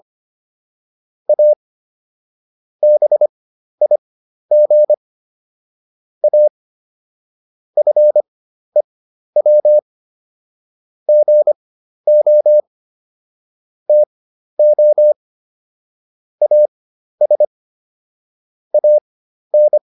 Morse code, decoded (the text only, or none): A BIG A FEW GO TO AS AN